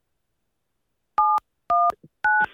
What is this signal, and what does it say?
DTMF, heard as telephone touch tones, tones of about 230 ms. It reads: 719